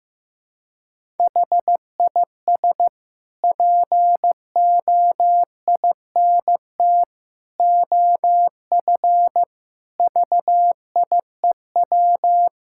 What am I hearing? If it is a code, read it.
Morse code, heard as a keyed tone, about 15 wpm, 700 Hz. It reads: HIS POINT OF VIEW
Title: Sentences from Top 1000 Words - Review Part 6 - 15wpm